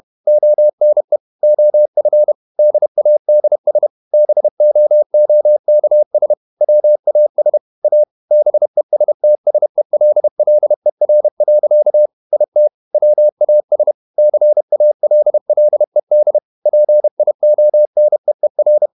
Morse code, heard as a keyed tone, about 31 wpm, 600 Hz. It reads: ONE OF DADS BOOKS WAS A BESTSELLER. IT WAS CALLED PIONEER
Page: Never 31wpm | CW med Gnister